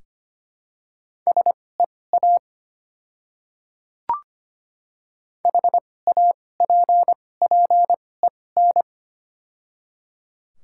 Morse code, heard as a keyed tone, about 25 wpm, spaced as for 18 wpm, 690 Hz. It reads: SEA HAPPEN
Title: 2x Character Spacing - Top 500 Words - 25wpm